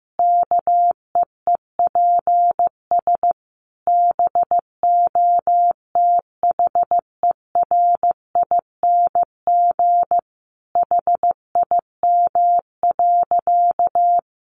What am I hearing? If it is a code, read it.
Morse code, heard as a keyed tone, about 15 wpm, 700 Hz. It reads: KEEPS BOTHERING HIM.
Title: Additional Easy Phrases 15wpm